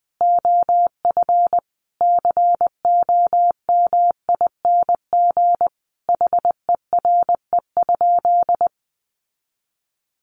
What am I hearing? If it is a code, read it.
Morse code, heard as a keyed tone, about 20 wpm, 700 Hz. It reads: OF COMING HERE?
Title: Sentences from Top 1300 Words 20wpm